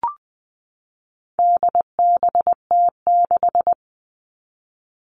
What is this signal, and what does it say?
Morse code, heard as a keyed tone, about 20 wpm, 700 Hz. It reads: DBT6
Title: All Cut Numbers - 4-Digit 20wpm